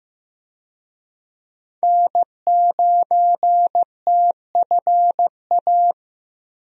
Morse code, heard as a keyed tone, about 15 wpm, 700 Hz. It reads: N9TFA
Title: USA Call Signs - 5 Characters - Repeated 3x - 15wpm